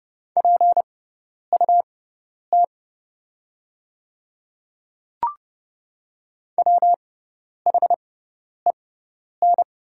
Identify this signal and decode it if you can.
Morse code, heard as a keyed tone, about 30 wpm, spaced as for 10 wpm, 690 Hz. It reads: PUT WHEN
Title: 6x Character Spacing - Top 200 Words - 30wpm